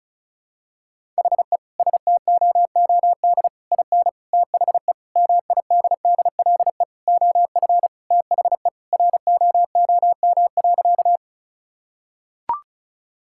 Morse code, heard as a keyed tone, about 34 wpm, 700 Hz. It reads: HE STOOD IN THE MIDDLE OF THE ROOM.
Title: Sentences from Top 1000 Words - Review Part 6 - 35wpm